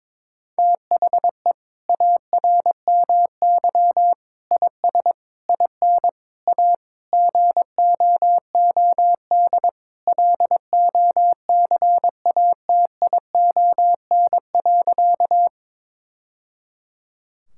Morse code, heard as a keyed tone, about 21 wpm, 700 Hz. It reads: THE ARMY IS IN A GOOD LOCATION.
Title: Sentences from Top 1000 Words - Review Part 6 - 22wpm